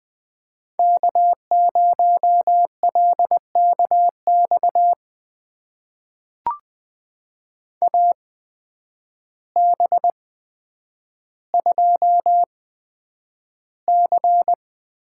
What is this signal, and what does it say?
Morse code, heard as a keyed tone, about 20 wpm, spaced as for 9 wpm, 700 Hz. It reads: K0LKX AB2C
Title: USA Call Signs - 5 Characters - 8x Character Spacing - 20wpm